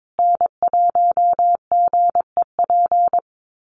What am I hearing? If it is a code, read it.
Morse code, heard as a keyed tone, about 22 wpm, 700 Hz. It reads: N1GEP
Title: USA Call Signs Repeated 3x 22wpm